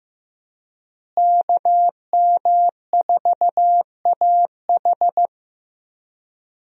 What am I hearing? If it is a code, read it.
Morse code, heard as a keyed tone, about 16 wpm, 700 Hz. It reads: KM4AH
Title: USA Call Signs - 5 Characters - Repeated 2x - 15wpm